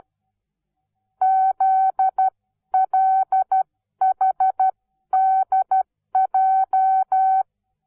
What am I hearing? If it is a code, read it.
Morse code, heard as a keyed tone, about 12 wpm, spaced as for 11 wpm, 760 Hz. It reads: ZLHDJ
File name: Onlinelehrgang / morsen / DJ4UF Morsen17